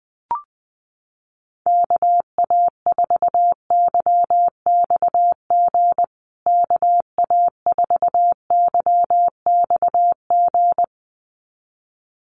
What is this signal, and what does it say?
Morse code, heard as a keyed tone, about 20 wpm, 700 Hz. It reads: KA4YXG KA4YXG
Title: USA Call Signs Repeated 2x 20wpm